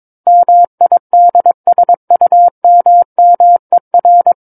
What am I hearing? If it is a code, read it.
Morse code, heard as a keyed tone, about 22 wpm, 700 Hz. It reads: MIDSUMMER
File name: Book One / 13-Chapter 13 - 22 WPM